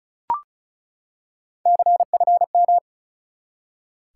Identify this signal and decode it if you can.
Morse code, heard as a keyed tone, about 36 wpm, 700 Hz. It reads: CFM
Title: Top 180 QSO Elements 35wpm